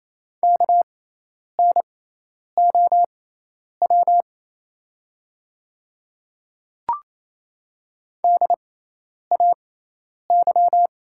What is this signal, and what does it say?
Morse code, heard as a keyed tone, about 28 wpm, spaced as for 10 wpm, 700 Hz. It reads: KNOW DAY
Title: 6x Character Spacing - Top 100 Words - 28wpm